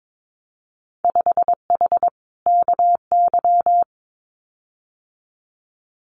Morse code, heard as a keyed tone, about 22 wpm, 700 Hz. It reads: 5H KY
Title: Winter Field Day Exchanges Encore 22wpm